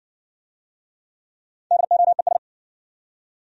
Morse code, heard as a keyed tone, about 57 wpm, 700 Hz. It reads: DOES